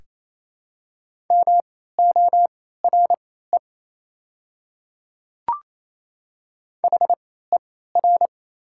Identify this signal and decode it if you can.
Morse code, heard as a keyed tone, about 27 wpm, spaced as for 16 wpm, 700 Hz. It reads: MORE HER
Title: 3x Character Spacing - Top 300 Words - 28wpm